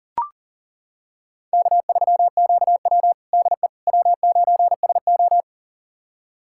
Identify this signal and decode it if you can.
Morse code, heard as a keyed tone, about 37 wpm, 700 Hz. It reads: K3QW DE W9SO
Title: USA Callsign DE Callsign - 4 Characters - 40wpm